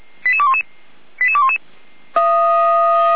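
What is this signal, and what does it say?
Five-tone selective call (ZVEI), heard as quick runs of five tones, tones of about 70 ms. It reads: 88211 88211
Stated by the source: ZVEI five-tone selective call 88129, received by SDR, from the feuerwehr-orlamuende.org audio server.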